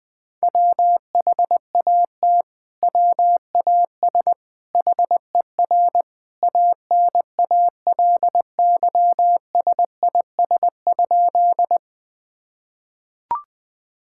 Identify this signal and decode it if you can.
Morse code, heard as a keyed tone, about 20 wpm, 700 Hz. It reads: WHAT WAS HER ANALYSIS?
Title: Sentences from Top 1000 Words - Review Part 5 - 20wpm